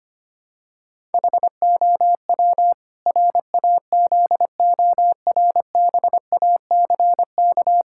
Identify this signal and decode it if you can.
Morse code, heard as a keyed tone, about 25 wpm, 700 Hz. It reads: HOW RAZORBACK
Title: Pangram Challenge v2 - 25wpm